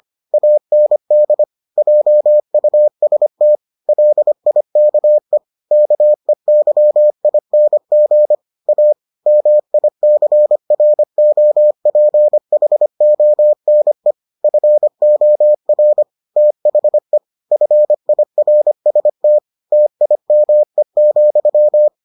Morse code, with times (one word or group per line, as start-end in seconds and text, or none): 0.3-1.4 AND
1.8-3.6 JUST
3.9-5.4 LIKE
5.7-8.3 KEYING
8.7-8.9 A
9.3-14.1 MICROPHONE
14.4-16.0 FOR
16.4-17.2 THE
17.5-19.4 FIRST
19.7-21.9 TIME,